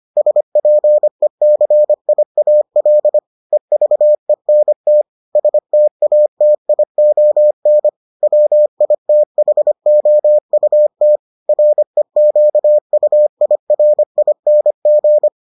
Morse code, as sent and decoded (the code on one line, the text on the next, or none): ... .--. -.-. .. .- .-.. / . ...- . -. - / ... - .- - .. --- -. / .-- .. - .... --- ..- - / .-. --.- ..- .. .-. .. -. --.
SPECIAL EVENT STATION WITHOUT REQUIRING